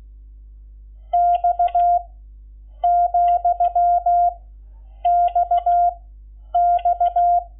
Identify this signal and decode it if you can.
Morse code, heard as a keyed tone, about 16 wpm, 700 Hz. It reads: X , X X
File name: Lecciones / CLASE Nº11